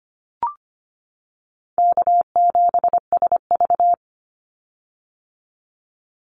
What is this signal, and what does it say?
Morse code, heard as a keyed tone, about 25 wpm, 700 Hz. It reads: K7SV
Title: Easy Call Signs - 25wpm